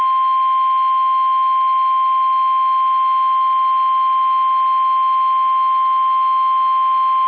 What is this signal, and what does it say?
Emergency Alert System (SAME) tones (the 1050 Hz attention tone). It is heard from 0.4 to 6.9 s.
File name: eas / KUSO / KUSO